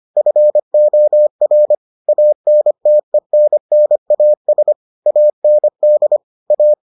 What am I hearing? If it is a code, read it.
Morse code, heard as a keyed tone, about 25 wpm, 600 Hz. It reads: FOR ANTENNAS AND A